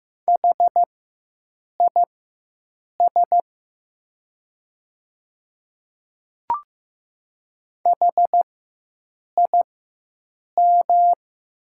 Morse code, heard as a keyed tone, about 15 wpm, spaced as for 7 wpm, 700 Hz. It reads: HIS HIM
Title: 4x Character Spacing - Top 100 Words - 15wpm